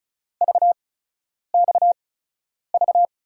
Morse code, heard as a keyed tone, about 36 wpm, spaced as for 10 wpm, 700 Hz. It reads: VXV
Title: USA Call Signs - 5 Characters - 8x Character Spacing - 35wpm